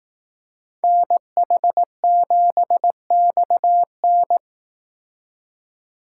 Morse code, heard as a keyed tone, about 19 wpm, 700 Hz. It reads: NH7XN
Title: Speed Racing - USA 2x2 Call Signs 15wpm